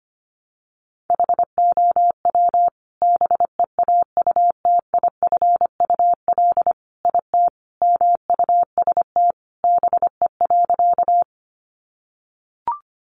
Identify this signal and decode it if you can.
Morse code, heard as a keyed tone, about 25 wpm, 700 Hz. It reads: HOW BEAUTIFUL IT MUST BE.
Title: Sentences from Top 900 Words 25wpm